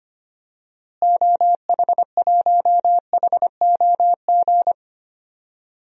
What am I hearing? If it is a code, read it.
Morse code, heard as a keyed tone, about 25 wpm, 700 Hz. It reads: OH1HOG
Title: DXCC-100 Call Signs - 6 Characters - Repeated 2x - 25wpm